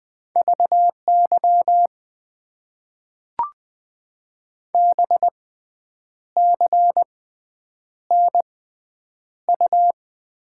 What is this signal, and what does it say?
Morse code, heard as a keyed tone, about 20 wpm, spaced as for 8 wpm, 700 Hz. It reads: VY BCNU
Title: Top 90 QSO Elements - 6x Character Spacing - 20wpm